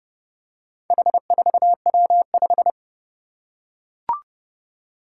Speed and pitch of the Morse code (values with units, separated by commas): 28 wpm, 700 Hz